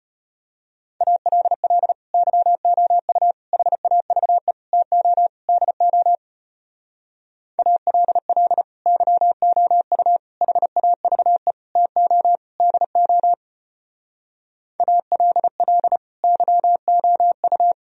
Morse code, heard as a keyed tone, about 33 wpm, 700 Hz. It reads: ALL YOU HAVE TO DO ALL YOU HAVE TO DO ALL YOU